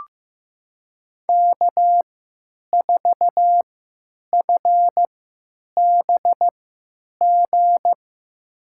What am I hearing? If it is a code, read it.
Morse code, heard as a keyed tone, about 15 wpm, spaced as for 9 wpm, 700 Hz. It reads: K4FBG
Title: USA Call Signs - 5 Characters - 3x Character Spacing - 15wpm